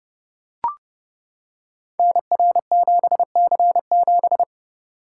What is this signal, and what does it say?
Morse code, heard as a keyed tone, about 28 wpm, 700 Hz. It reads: NR7C7
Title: Code Groups - 5 Letter:Numbers - 30wpm